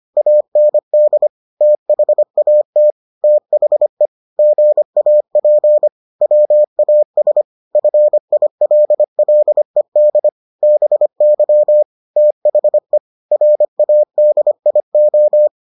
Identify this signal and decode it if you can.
Morse code, heard as a keyed tone, about 25 wpm, 600 Hz. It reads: AND THAT THE GAP WAS FILLED BY THE RADIO